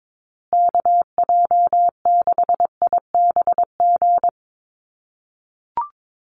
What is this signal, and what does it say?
Morse code, heard as a keyed tone, about 22 wpm, 700 Hz. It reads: KJ6IBG
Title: USA Call Signs Repeated 2x 22wpm